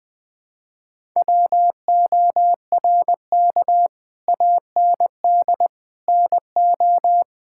Morse code, heard as a keyed tone, about 20 wpm, 700 Hz. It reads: WORK AND NO
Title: Sentences with 8-Word Phrases - 20wpm